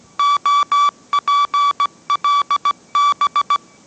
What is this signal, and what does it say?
Morse code, heard as a keyed tone, about 17 wpm, 1150 Hz. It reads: OPLB
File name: mrx / mrx080 18wpm